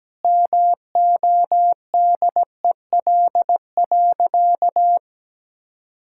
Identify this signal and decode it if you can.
Morse code, heard as a keyed tone, about 17 wpm, 700 Hz. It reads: MODEL.
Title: Sentences from Top 1000 Words - Review Part 5 - 17wpm